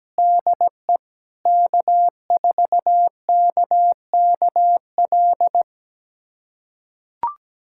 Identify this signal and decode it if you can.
Morse code, heard as a keyed tone, about 17 wpm, 700 Hz. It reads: DE K4KKL
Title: USA Callsign DE Callsign - 5 Characters - 17wpm